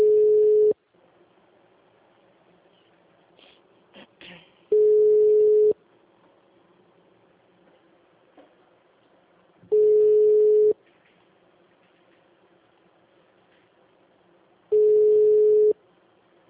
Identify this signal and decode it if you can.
Ringback tone (Europe, 425 Hz).
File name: out- -2013- - - .214